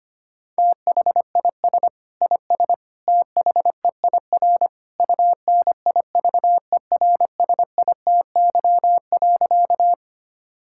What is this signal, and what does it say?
Morse code, heard as a keyed tone, about 25 wpm, 700 Hz. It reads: THIS IS THEIR UNIVERSITY.
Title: Sentences from Top 1000 Words - Review Part 5 - 25wpm